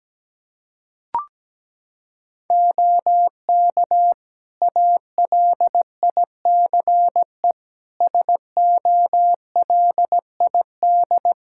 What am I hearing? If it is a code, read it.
Morse code, heard as a keyed tone, about 17 wpm, 700 Hz. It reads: OK ALICE SOLID
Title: Practice QSO - Exchange 3 - More Info - 17wpm